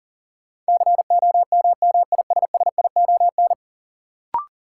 Morse code, heard as a keyed tone, about 40 wpm, 700 Hz. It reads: COMMISSION
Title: Sentences from Top 1400 Words - New Word Drill Alt - 40wpm